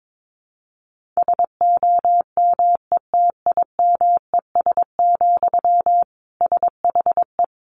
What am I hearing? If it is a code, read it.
Morse code, heard as a keyed tone, about 22 wpm, 700 Hz. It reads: SOMETIMES, SHE
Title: Sentences from Top 1000 Words 22wpm